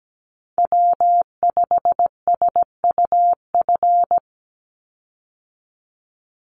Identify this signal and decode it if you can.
Morse code, heard as a keyed tone, about 17 wpm, 700 Hz. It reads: W5SUF
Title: USA Call Signs 17wpm